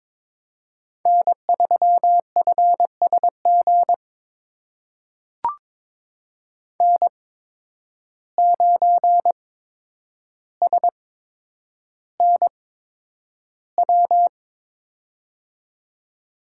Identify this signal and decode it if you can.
Morse code, heard as a keyed tone, about 21 wpm, spaced as for 9 wpm, 700 Hz. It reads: N3FSG N9SNW
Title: USA Call Signs - 5 Characters - 8x Character Spacing - 22wpm